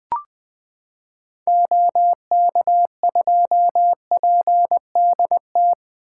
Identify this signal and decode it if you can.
Morse code, heard as a keyed tone, about 20 wpm, 700 Hz. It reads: OK2PDT
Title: 2023 SOTA Chaser Call Signs - 20wpm